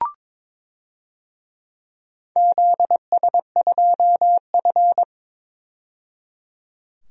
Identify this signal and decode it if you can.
Morse code, heard as a keyed tone, about 22 wpm, 700 Hz. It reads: ZS2F
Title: DXCC-100 Call Signs - 4 Characters - 22wpm